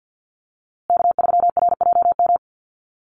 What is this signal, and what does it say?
Morse code, heard as a keyed tone, about 50 wpm, 700 Hz. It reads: K3RJM
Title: USA Call Signs Repeated 2x 50wpm